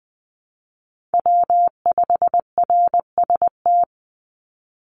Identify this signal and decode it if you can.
Morse code, heard as a keyed tone, about 19 wpm, 700 Hz. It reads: W5RST
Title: NA SOTA Chaser Callsigns 20wpm